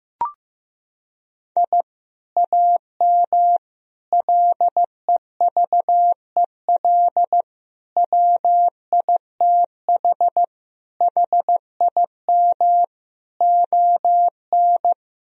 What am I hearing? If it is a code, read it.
Morse code, heard as a keyed tone, about 16 wpm, 700 Hz. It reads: I AM LEVEL WITH HIM ON
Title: Sentences from Top 1000 Words - Review Part 4 - 15wpm